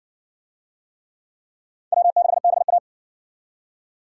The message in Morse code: W6BA

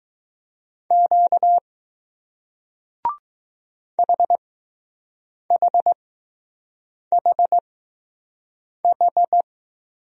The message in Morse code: Q HHHH

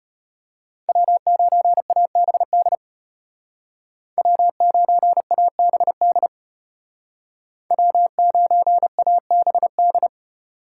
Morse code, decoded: W9ABD W9ABD W9ABD

